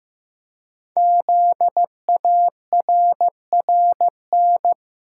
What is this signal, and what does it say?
Morse code, heard as a keyed tone, about 15 wpm, 700 Hz. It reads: ZARRN
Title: Code Groups - 5 Letters - 15wpm